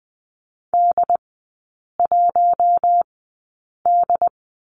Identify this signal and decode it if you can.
Morse code, heard as a keyed tone, about 20 wpm, spaced as for 8 wpm, 700 Hz. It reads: D1D
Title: 067 Lesson 12e - Letter d - Three Characters 20wpm